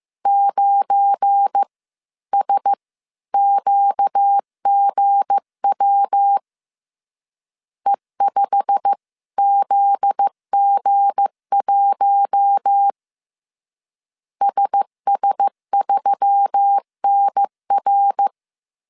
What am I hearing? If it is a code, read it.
Morse code, heard as a keyed tone, about 15 wpm, spaced as for 13 wpm, 800 Hz. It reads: 9SQGW E5ZG1 SS3NR
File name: CW / DET / 1 9 1500C S C 14wpm 3245